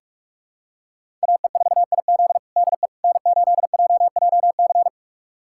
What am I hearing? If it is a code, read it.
Morse code, heard as a keyed tone, about 46 wpm, 700 Hz. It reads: AE4IZ DE N8JJC
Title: USA Callsign DE Callsign - 5 Characters - 45wpm